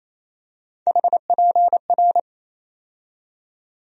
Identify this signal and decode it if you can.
Morse code, heard as a keyed tone, about 28 wpm, 700 Hz. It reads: HPR
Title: USA Call Sign Postfixes - 3 Letters 28wpm